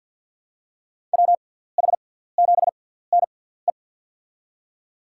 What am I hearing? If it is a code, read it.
Morse code, heard as a keyed tone, about 52 wpm, spaced as for 17 wpm, 700 Hz. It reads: WH7NE
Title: USA Call Signs - 5 Characters - 6x Character Spacing - 50wpm